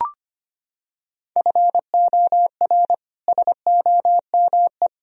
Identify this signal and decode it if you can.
Morse code, heard as a keyed tone, about 25 wpm, 700 Hz. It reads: FOR SOME
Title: Sentences from Top 1000 Words - Review Part 3 - 25wpm